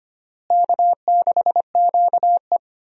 Morse code, heard as a keyed tone, about 25 wpm, 700 Hz. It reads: K6QE
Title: USA Call Signs - 4 Characters - Repeated 3x - 25wpm